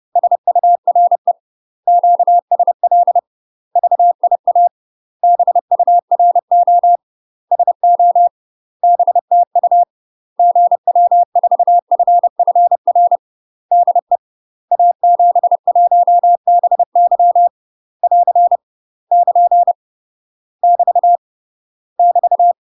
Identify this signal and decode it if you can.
Morse code, heard as a keyed tone, about 30 wpm, 700 Hz. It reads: SURE QSL VIA BURO SO BTU GW4FFR DE A71BY <AR> <KN> <BT> <BT>
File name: UK MASTER QSOs TRAINING / 30WPM / UK MASTER QSOs 0023